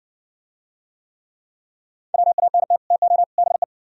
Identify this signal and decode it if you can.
Morse code, heard as a keyed tone, about 56 wpm, 700 Hz. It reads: WANT TO BE